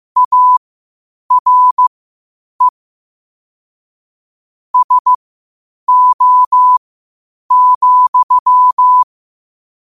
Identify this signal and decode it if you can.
Morse code, heard as a keyed tone, about 15 wpm, spaced as for 8 wpm, 1000 Hz. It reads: ARE SO,
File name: qotdfiles / 8wpm